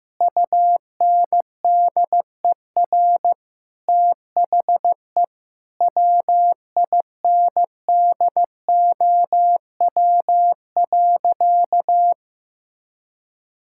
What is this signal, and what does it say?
Morse code, heard as a keyed tone, about 14 wpm, 700 Hz. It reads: UNDER THE WINDOW.
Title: Sentences from Top 1600 Words 15wpm